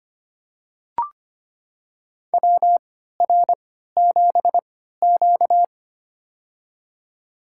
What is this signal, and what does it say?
Morse code, heard as a keyed tone, about 25 wpm, spaced as for 14 wpm, 700 Hz. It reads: WR7Q
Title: USA Call Signs - 4 Characters - 3x Character Spacing - 25wpm